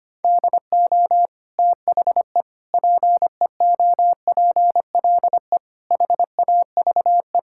Morse code, heard as a keyed tone, about 25 wpm, 700 Hz. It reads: DO THE PEOPLE HAVE
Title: Sentences from Top 1000 Words - Review Part 5 - 25wpm